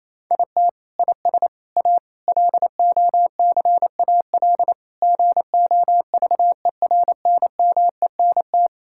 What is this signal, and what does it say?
Morse code, heard as a keyed tone, about 27 wpm, 700 Hz. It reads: IT IS A LOCAL GOVERNMENT
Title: Sentences from Top 1000 Words - Review Part 5 - 28wpm